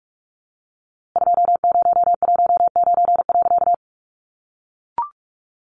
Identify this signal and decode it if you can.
Morse code, heard as a keyed tone, about 45 wpm, 700 Hz. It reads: 2019.
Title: A Sound Of Thunder 45wpm